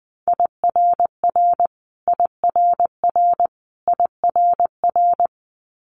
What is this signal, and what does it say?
Morse code, heard as a keyed tone, about 20 wpm, 700 Hz. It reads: IRR IRR IRR
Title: Multi-letter Phonemes Repeated 3x 20wpm